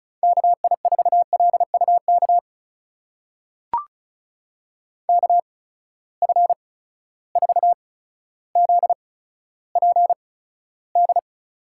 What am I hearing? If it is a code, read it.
Morse code, heard as a keyed tone, about 36 wpm, spaced as for 15 wpm, 700 Hz. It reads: KI4LUK KF4ZPD
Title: USA Call Signs - 6 Characters - 8x Character Spacing - 35wpm